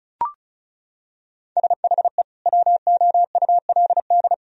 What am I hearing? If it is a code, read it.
Morse code, heard as a keyed tone, about 36 wpm, 700 Hz. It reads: SHE WOULD
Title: Sentences from Top 1000 Words - Review Part 4 - 35wpm